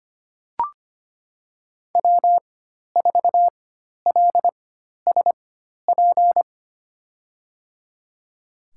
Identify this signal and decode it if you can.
Morse code, heard as a keyed tone, about 25 wpm, spaced as for 12 wpm, 700 Hz. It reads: W4LSP
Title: USA Call Signs - 5 Characters - 4x Character Spacing - 25wpm